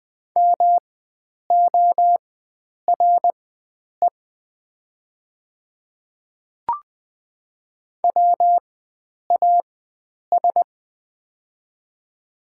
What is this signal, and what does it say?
Morse code, heard as a keyed tone, about 20 wpm, spaced as for 9 wpm, 700 Hz. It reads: MORE WAS
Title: 4x Character Spacing - Top 300 Words - 20wpm